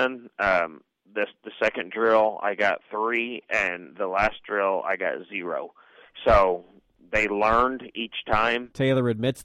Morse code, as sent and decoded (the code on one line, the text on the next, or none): none